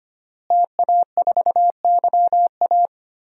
Morse code, heard as a keyed tone, about 25 wpm, 700 Hz. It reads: TA4YA
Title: DXCC-100 Call Signs - 5 Characters - Repeated 3x - 25wpm